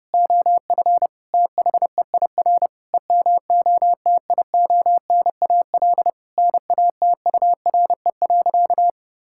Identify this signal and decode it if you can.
Morse code, heard as a keyed tone, about 32 wpm, 700 Hz. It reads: OF THEIR EMOTIONAL NATURE.
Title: Sentences from Top 1000 Words - Review Part 5 - 30wpm